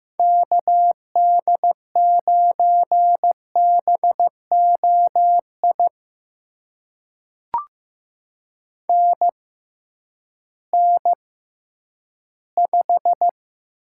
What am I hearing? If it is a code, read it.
Morse code, heard as a keyed tone, about 15 wpm, spaced as for 10 wpm, 700 Hz. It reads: KD9BOI NN5
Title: USA Call Signs - 6x Character Spacing - 15wpm